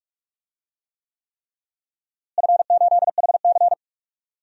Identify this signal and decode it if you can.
Morse code, heard as a keyed tone, about 46 wpm, 700 Hz. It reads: F8HC
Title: DXCC-100 Call Signs - 4 Characters - Repeated 3x - 45wpm